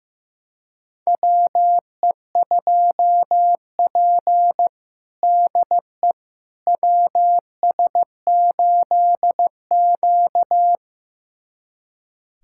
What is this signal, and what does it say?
Morse code, heard as a keyed tone, about 15 wpm, 700 Hz. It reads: WE2P DE WS8Q
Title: USA Callsign DE Callsign - 4 Characters - 15wpm